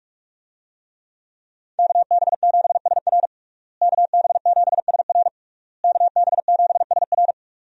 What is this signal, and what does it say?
Morse code, heard as a keyed tone, about 46 wpm, 700 Hz. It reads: KB7SR KB7SR KB7SR
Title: USA Call Signs - 5 Characters - Repeated 3x - 45wpm